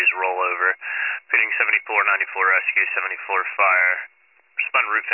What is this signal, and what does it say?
Emergency Alert System (SAME) tones, heard as harsh buzzing bursts, with speech. It is heard from 0.7 to 4.1 s.